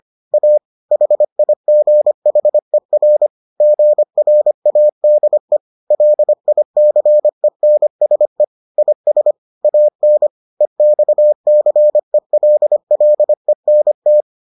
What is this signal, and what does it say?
Morse code, heard as a keyed tone, about 25 wpm, 600 Hz. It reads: A HIGHER GRADE LICENSE IS AN EXCELLENT